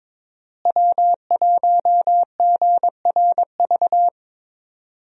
Morse code, heard as a keyed tone, about 22 wpm, 700 Hz. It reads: W1GRV
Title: USA Call Signs - 5 Characters - Repeated 2x - 22wpm